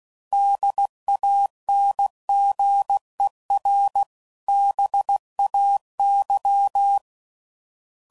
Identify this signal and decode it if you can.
Morse code, heard as a keyed tone, about 16 wpm, 800 Hz. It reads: DANGER BAY
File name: CW Code Groups @15WPM